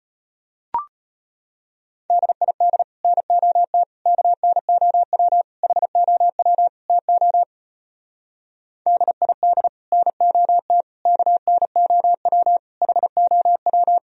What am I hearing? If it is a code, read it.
Morse code, heard as a keyed tone, about 33 wpm, 700 Hz. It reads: DID NOT KNOW HOW TO DID NOT KNOW HOW